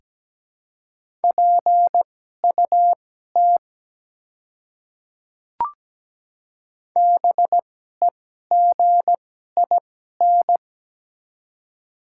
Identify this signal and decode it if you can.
Morse code, heard as a keyed tone, about 17 wpm, spaced as for 12 wpm, 700 Hz. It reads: PUT BEGIN
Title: 2x Character Spacing - Top 300 Words - 17wpm